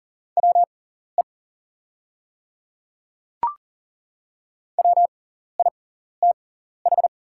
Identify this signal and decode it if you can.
Morse code, heard as a keyed tone, about 40 wpm, spaced as for 14 wpm, 700 Hz. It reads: WE WITH
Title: 6x Character Spacing - Top 100 Words - 40wpm